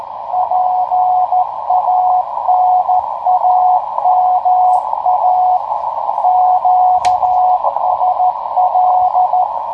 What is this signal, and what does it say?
Morse code, heard as a keyed tone, about 14 wpm, 730 Hz, with radio noise. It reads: PANAMA QSL
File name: tracks / hp1rcp